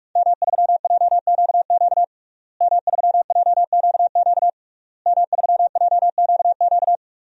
Morse code, heard as a keyed tone, about 46 wpm, 700 Hz. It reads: M3JQQ M3JQQ M3JQQ
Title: DXCC-100 Call Signs - 5 Characters - Repeated 3x - 45wpm